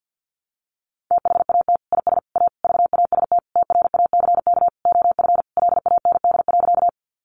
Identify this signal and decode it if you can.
Morse code, heard as a keyed tone, about 50 wpm, 700 Hz. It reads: THAT IS A VAST TRACK OF LAND.
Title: Sentences from Top 1300 Words 50wpm